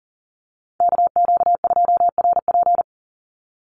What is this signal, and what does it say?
Morse code, heard as a keyed tone, about 40 wpm, 700 Hz. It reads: KQ2RP
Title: NA SOTA Chaser Callsigns 40wpm